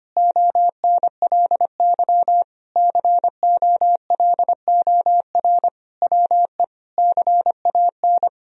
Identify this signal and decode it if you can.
Morse code, heard as a keyed tone, about 25 wpm, 700 Hz. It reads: ONLY COLOR WE CAN
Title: Sentences from Top 1000 Words - Review Part 3 - 25wpm